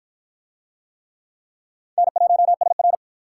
Morse code, heard as a keyed tone, about 52 wpm, 700 Hz. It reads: N1SR